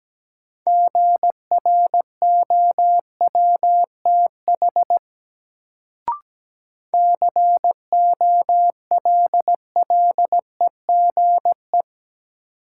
Morse code, heard as a keyed tone, about 17 wpm, 700 Hz. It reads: GROWTH COLLEGE
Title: Sentences from Top 1300 Words - New Word Drill Alt - 17wpm